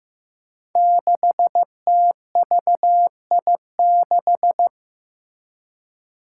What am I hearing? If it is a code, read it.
Morse code, heard as a keyed tone, about 15 wpm, 700 Hz. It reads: 6TVI6
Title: Five Challenging Letter-Number Combinations 15wpm